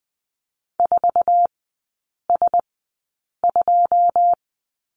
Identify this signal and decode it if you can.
Morse code, heard as a keyed tone, about 20 wpm, spaced as for 8 wpm, 700 Hz. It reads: 4S2
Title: 093 Lesson 16e - Letter c - Three Characters 20wpm